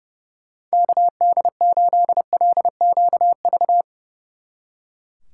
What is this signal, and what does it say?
Morse code, heard as a keyed tone, about 35 wpm, 700 Hz. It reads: KD8LQV
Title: Speed Racing - USA 2x3 Call Signs 30wpm